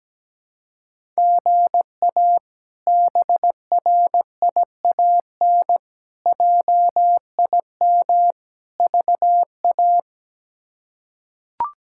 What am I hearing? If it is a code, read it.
Morse code, heard as a keyed tone, about 17 wpm, 700 Hz. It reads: GA BRIAN JIM VA
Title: K1USN SST QSO Practice - 17wpm